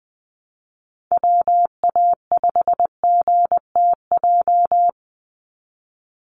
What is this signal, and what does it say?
Morse code, heard as a keyed tone, about 20 wpm, 700 Hz. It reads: WA5GTJ
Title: USA Call Signs 20wpm